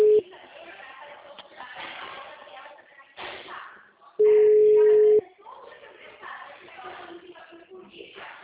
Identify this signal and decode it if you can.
Ringback tone (Europe, 425 Hz), with speech.